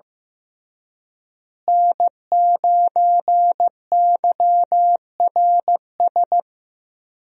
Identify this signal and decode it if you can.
Morse code, heard as a keyed tone, about 15 wpm, 700 Hz. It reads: N9YRS
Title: USA Call Signs - 5 Characters - Repeated 3x - 15wpm